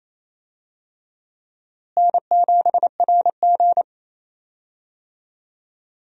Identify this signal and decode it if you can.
Morse code, heard as a keyed tone, about 27 wpm, 700 Hz. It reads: N7RG
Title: USA Call Signs - 4 Characters - Repeated 2x - 28wpm